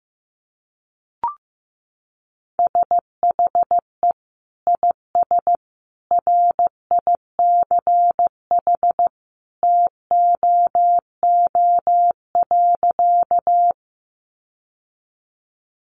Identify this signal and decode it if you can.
Morse code, heard as a keyed tone, about 15 wpm, 700 Hz. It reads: SHE IS RICH TOO.
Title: Sentences from Top 1100 Words 15wpm